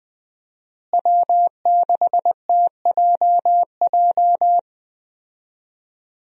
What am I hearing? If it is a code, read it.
Morse code, heard as a keyed tone, about 20 wpm, 700 Hz. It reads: W6TJJ
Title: USA Call Signs - 5 Characters - Repeated 3x - 20wpm